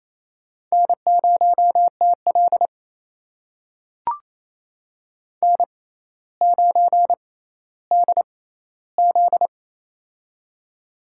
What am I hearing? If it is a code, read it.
Morse code, heard as a keyed tone, about 28 wpm, spaced as for 14 wpm, 700 Hz. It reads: N0TL N9DZ
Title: USA Call Signs - 4 Characters - 6x Character Spacing - 28wpm